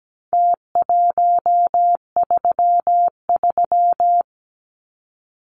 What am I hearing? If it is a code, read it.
Morse code, heard as a keyed tone, about 17 wpm, 700 Hz. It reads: T133
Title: Numbers - Contest Style - 4-Digit 17wpm